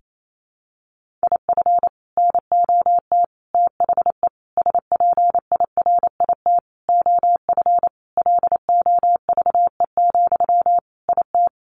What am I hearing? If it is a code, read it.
Morse code, heard as a keyed tone, about 28 wpm, 700 Hz. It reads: IF NOT THE SPIRIT OF LOVE, IT
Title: Sentences from Top 1400 Words 28wpm